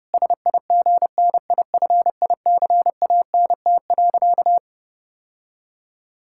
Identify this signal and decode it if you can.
Morse code, heard as a keyed tone, about 32 wpm, 700 Hz. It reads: SIGNIFICANT.
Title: Sentences from Top 1000 Words - Review Part 6 - 30wpm